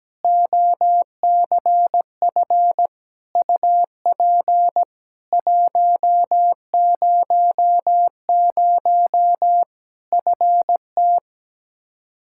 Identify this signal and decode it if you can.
Morse code, heard as a keyed tone, about 17 wpm, 700 Hz. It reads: OCF UP 100 FT
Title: K1USN SST QSO Practice 17wpm